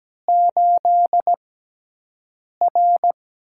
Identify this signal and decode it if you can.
Morse code, heard as a keyed tone, about 17 wpm, spaced as for 6 wpm, 700 Hz. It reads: 8R
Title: USA Call Signs - 6x Character Spacing - 17wpm